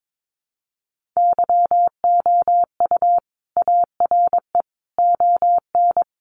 Morse code, heard as a keyed tone, about 22 wpm, 700 Hz. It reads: YOU ARE ON
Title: Sentences from Top 400 Words 22wpm